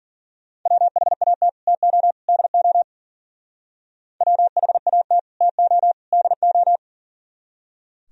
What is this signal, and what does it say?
Morse code, heard as a keyed tone, about 51 wpm, 700 Hz. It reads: WHAT TO DO WHAT TO DO